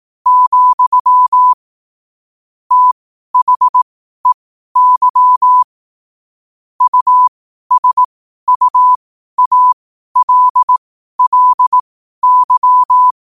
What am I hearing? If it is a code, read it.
Morse code, heard as a keyed tone, about 18 wpm, spaced as for 12 wpm, 1000 Hz. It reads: , THEY USUALLY
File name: qotdfiles / 12wpm